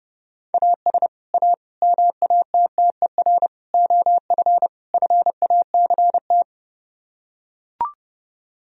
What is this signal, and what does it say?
Morse code, heard as a keyed tone, about 30 wpm, 700 Hz. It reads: AS A MATTER OF FACT